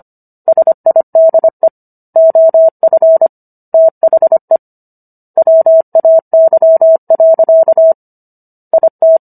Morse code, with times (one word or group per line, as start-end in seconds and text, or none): 0.5-1.7 SIDE
2.2-3.3 OF
3.7-4.6 THE
5.4-7.9 WAY.
8.7-9.2 IT